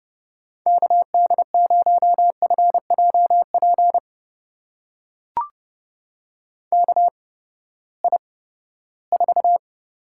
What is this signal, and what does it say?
Morse code, heard as a keyed tone, about 31 wpm, spaced as for 17 wpm, 700 Hz. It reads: KD0FJP KI4
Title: USA Call Signs - 6 Characters - 8x Character Spacing - 30wpm